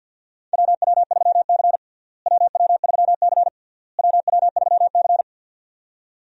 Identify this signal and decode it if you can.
Morse code, heard as a keyed tone, about 53 wpm, 700 Hz. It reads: WW3C WW3C WW3C
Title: USA Call Signs - 4 Characters - Repeated 3x - 50wpm